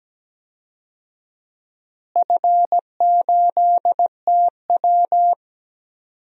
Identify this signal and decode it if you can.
Morse code, heard as a keyed tone, about 17 wpm, 700 Hz. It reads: F8TW
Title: DXCC-100 Call Signs - 4 Characters - Repeated 2x - 17wpm